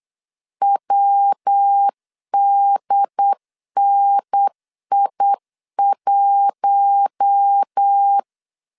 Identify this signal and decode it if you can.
Morse code, heard as a keyed tone, about 8 wpm, 800 Hz. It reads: WDNI1
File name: LEZIONI / 7 GF9Z / 7 GF9Z 8wpm 3213